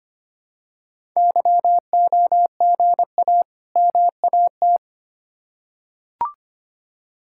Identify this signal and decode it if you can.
Morse code, heard as a keyed tone, about 25 wpm, 690 Hz. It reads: YOGA MAT
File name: practice / 25wpm / Easy Words with Hints - 25wpm